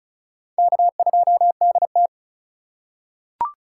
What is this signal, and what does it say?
Morse code, heard as a keyed tone, about 34 wpm, 700 Hz. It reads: K2DT
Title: USA Call Signs - 2x Character Spacing - 35wpm